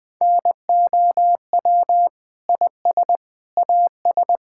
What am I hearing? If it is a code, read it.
Morse code, heard as a keyed tone, about 21 wpm, 700 Hz. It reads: NOW IS AS
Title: Sentences with 7-Word Phrases - 20wpm